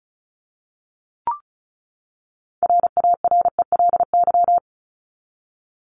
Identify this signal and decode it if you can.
Morse code, heard as a keyed tone, about 34 wpm, 700 Hz. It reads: RARELY
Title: LY Suffix - Top 100 Words with Base Words 35wpm